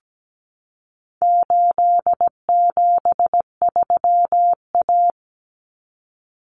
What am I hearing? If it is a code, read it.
Morse code, heard as a keyed tone, about 17 wpm, 700 Hz. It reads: 873A
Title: Common Cut Numbers - 4-Digit 17wpm